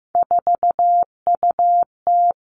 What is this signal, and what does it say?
Morse code, heard as a keyed tone, about 15 wpm, 700 Hz. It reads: 4UT
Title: Three Challenging Letter-Number Combinations 15wpm